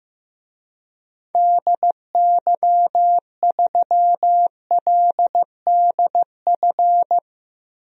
Morse code, heard as a keyed tone, about 15 wpm, 700 Hz. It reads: DY3LDF
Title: DXCC-100 Call Signs - 6 Characters - 15wpm